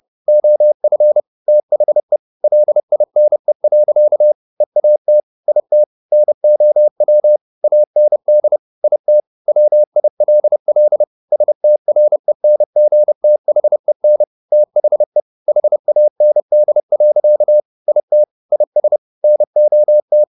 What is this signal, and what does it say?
Morse code, heard as a keyed tone, about 30 wpm, 600 Hz. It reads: OF THE LINE. EAT IT NOW AND IT WILL STRENGTHEN THE HAND. IT IS NOT
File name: the old man and the sea / the old man and the sea 1 30wpm